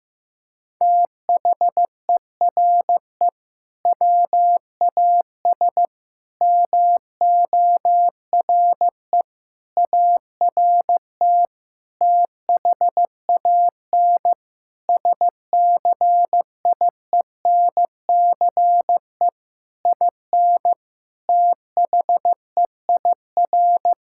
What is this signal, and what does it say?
Morse code, heard as a keyed tone, about 15 wpm, 700 Hz. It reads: THERE WAS MORE ART THAN SCIENCE IN THEIR
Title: Sentences from Top 1000 Words - Review Part 6 - 15wpm